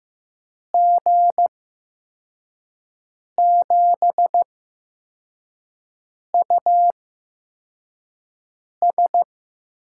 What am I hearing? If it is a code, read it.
Morse code, heard as a keyed tone, about 15 wpm, spaced as for 4 wpm, 700 Hz. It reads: G7US
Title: USA Call Signs - 6 Characters - 8x Character Spacing - 15wpm